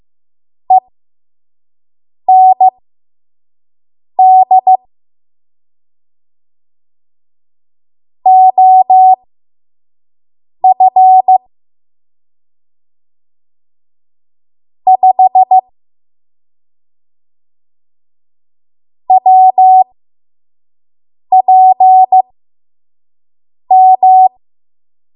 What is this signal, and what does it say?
Morse code, heard as a keyed tone, about 15 wpm, spaced as for 5 wpm, 750 Hz. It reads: END OF 5 WPM
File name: Archive / 5 WPM / 05WPM